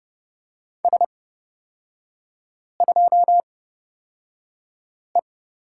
Morse code, heard as a keyed tone, about 30 wpm, spaced as for 5 wpm, 700 Hz. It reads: S2E
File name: Lesson 38e - Letter q - Three Characters 30wpm